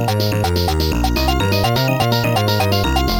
There is no code.